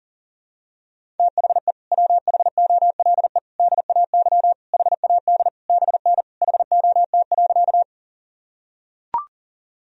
Morse code, heard as a keyed tone, about 40 wpm, 700 Hz. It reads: THE WHOLE DAY HAD BN HOT.
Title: Sentences from Top 500 Words and Abbreviations 40wpm